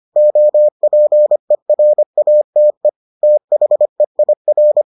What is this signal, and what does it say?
Morse code, heard as a keyed tone, about 25 wpm, 600 Hz. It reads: OPERATE THEIR